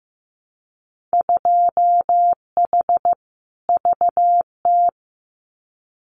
Morse code, heard as a keyed tone, about 15 wpm, 700 Hz. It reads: 2H VT
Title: Winter Field Day Exchanges 15wpm